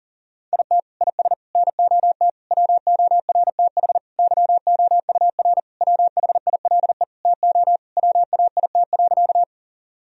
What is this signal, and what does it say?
Morse code, heard as a keyed tone, about 37 wpm, 700 Hz. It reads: IT IS NOT WORTH YOUR WHILE TO WAIT.
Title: Sentences from Top 1000 Words - Review Part 1 - 40wpm